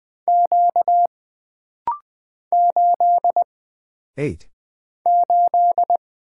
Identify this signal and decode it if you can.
Morse code, heard as a keyed tone, about 20 wpm, 700 Hz. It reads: Q 8 8
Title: Single Character (Letter:Number:Punctuation:Prosign) - Rapid-Fire - 20wpm